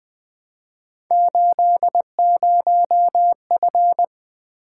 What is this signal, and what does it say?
Morse code, heard as a keyed tone, about 20 wpm, 700 Hz. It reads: 80F
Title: Three Letter-Number Combinations - 20wpm